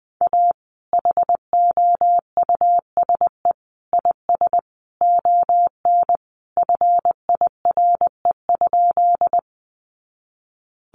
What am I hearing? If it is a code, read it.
Morse code, heard as a keyed tone, about 20 wpm, 700 Hz. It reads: A HOUSE IS ON FIRE?
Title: Sentences from Top 400 Words 20wpm